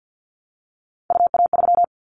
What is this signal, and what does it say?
Morse code, heard as a keyed tone, about 50 wpm, 700 Hz. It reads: UA3
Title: Callsign Prefixes for Top 25 DXCC - Repeated 3x - 50wpm